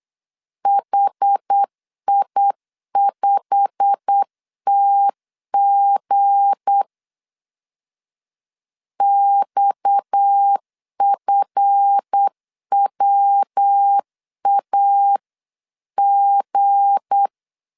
Morse code, heard as a keyed tone, about 8 wpm, 800 Hz. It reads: HI5TG XFWAG
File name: CW / DET / 1 9 1500C S B 8wpm 5624